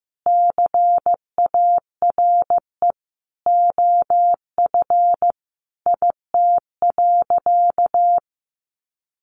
Morse code, heard as a keyed tone, about 15 wpm, 700 Hz. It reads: CARE OF IT.
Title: Additional Easy Phrases 15wpm